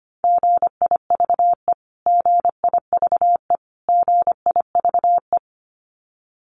Morse code, heard as a keyed tone, about 25 wpm, 700 Hz. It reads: GIVE GIVE GIVE
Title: Top 500 Words Repeated 25wpm